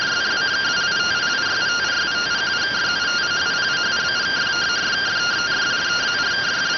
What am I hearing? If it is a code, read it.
Two-tone teleprinter signal, about 170 Hz shift.